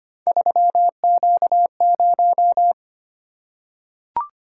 Morse code, heard as a keyed tone, about 25 wpm, 700 Hz. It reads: 3Q0
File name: Three Letter-Number Combinations - 25wpm